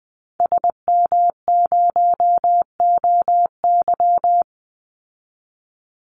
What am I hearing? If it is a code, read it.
Morse code, heard as a keyed tone, about 20 wpm, 700 Hz. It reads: SM0OY
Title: Call Signs Repeated 2x - 20wpm